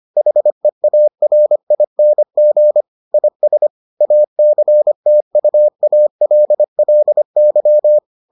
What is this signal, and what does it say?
Morse code, heard as a keyed tone, about 25 wpm, 600 Hz. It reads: HEARING IS ACTUALLY